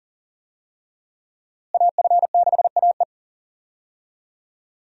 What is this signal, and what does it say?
Morse code, heard as a keyed tone, about 40 wpm, 700 Hz. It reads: AF6AE
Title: USA Call Signs - 5 Characters - Repeated 3x - 40wpm